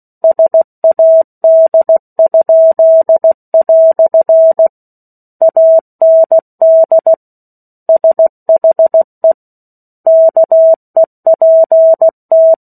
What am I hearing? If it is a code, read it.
Morse code, heard as a keyed tone, about 16 wpm, 650 Hz. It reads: SAD?" AND SHE KEPT
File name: 03-Chapter-03 16 WPM